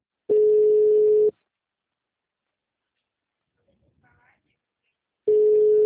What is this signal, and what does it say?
Ringback tone (Europe, 425 Hz), with speech.
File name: out- -2014